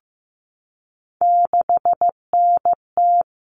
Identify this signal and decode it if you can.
Morse code, heard as a keyed tone, about 15 wpm, 700 Hz. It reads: 6NT
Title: All Cut Numbers - 3-Digit 15wpm